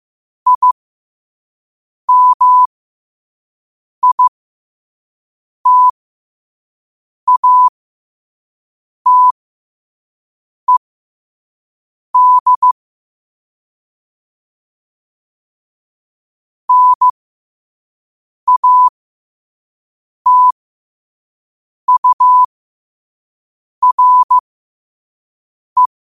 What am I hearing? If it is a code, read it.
Morse code, heard as a keyed tone, about 15 wpm, spaced as for 5 wpm, 1000 Hz. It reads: IMITATED NATURE
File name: qotdfiles / 5wpm